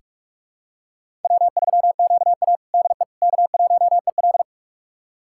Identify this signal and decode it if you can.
Morse code, heard as a keyed tone, about 43 wpm, 700 Hz. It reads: W3QA DE K1EL
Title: USA Callsign DE Callsign - 4 Characters - 45wpm